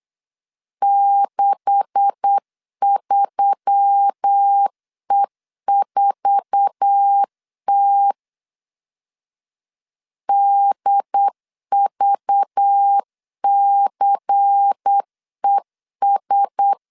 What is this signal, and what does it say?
Morse code, heard as a keyed tone, about 8 wpm, 800 Hz. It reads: 63E4T DVCES
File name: CW / DET / 1 9 1500C S B 8wpm 5624